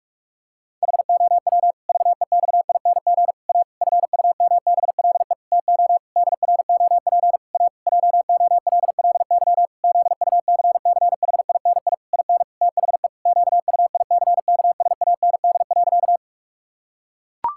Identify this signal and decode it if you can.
Morse code, heard as a keyed tone, about 44 wpm, 700 Hz. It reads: HOW VEXING A FUMBLE TO DROP A JOLLY ZUCCHINI IN THE QUICKSAND.
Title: Pangram Challenge v2 - 45wpm